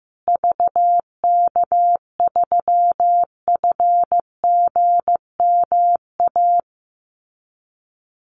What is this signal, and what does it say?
Morse code, heard as a keyed tone, about 15 wpm, 700 Hz. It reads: VK3FGMA
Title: USA Call Signs Repeated 2x 15wpm